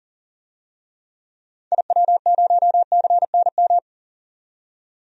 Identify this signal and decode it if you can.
Morse code, heard as a keyed tone, about 40 wpm, 700 Hz. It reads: IW0CNM
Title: DXCC-100 Call Signs - 6 Characters - 40wpm